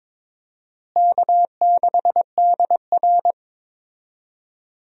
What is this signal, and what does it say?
Morse code, heard as a keyed tone, about 21 wpm, 700 Hz. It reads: K6DR
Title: USA Call Signs - 4 Characters - Repeated 2x - 22wpm